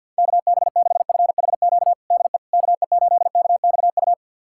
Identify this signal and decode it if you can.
Morse code, heard as a keyed tone, about 52 wpm, 700 Hz. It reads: KB6FHQ DE KE8KXU
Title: USA Callsign DE Callsign 50wpm